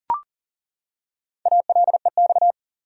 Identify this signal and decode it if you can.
Morse code, heard as a keyed tone, about 37 wpm, 700 Hz. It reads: ALEX
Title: USA Common Names - Max 5 Letters - 40wpm